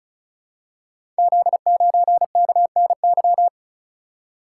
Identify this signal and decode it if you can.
Morse code, heard as a keyed tone, about 36 wpm, 700 Hz. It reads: Z9KNY
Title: Code Groups - 5 Letter:Numbers - 35wpm